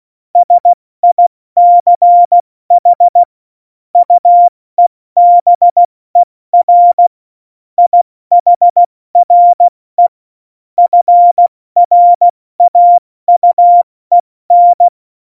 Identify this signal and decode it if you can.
Morse code, heard as a keyed tone, about 16 wpm, 700 Hz. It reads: SICH UEBER IHRE FRAUEN
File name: Morsekurs / Witze / Witz 23. Juli 2019, 70 F80